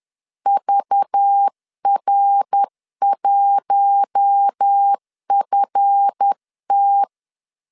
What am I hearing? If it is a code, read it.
Morse code, heard as a keyed tone, about 10 wpm, 800 Hz. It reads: VR1FT
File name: LEZIONI / 7 GF9Z / 7 GF 10wpm 2514